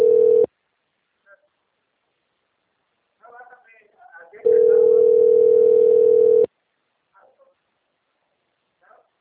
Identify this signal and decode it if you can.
Ringback tone (North America), 440 + 480 Hz.
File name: out- - - - - .1685